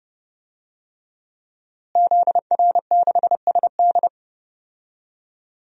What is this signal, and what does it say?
Morse code, heard as a keyed tone, about 31 wpm, 700 Hz. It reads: ZR6SD
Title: DXCC-100 Call Signs - 5 Characters - Repeated 3x - 30wpm